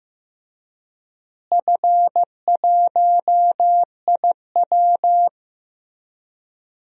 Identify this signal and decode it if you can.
Morse code, heard as a keyed tone, about 15 wpm, 700 Hz. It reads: F1IW
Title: DXCC-100 Call Signs - 4 Characters - Repeated 3x - 15wpm